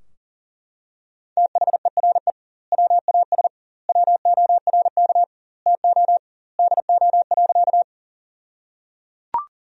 Morse code, heard as a keyed tone, about 37 wpm, 700 Hz. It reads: THERE WAS WORK TO DO.
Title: 2x Word-Spacing - Sentences from Top 200 Words - 40wpm